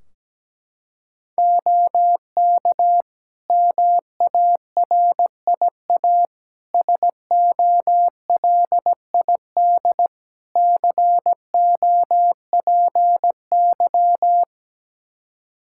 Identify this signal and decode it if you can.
Morse code, heard as a keyed tone, about 17 wpm, 700 Hz. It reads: OK MARIA SOLID COPY
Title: Practice QSO - Exchange 3 - More Info - 17wpm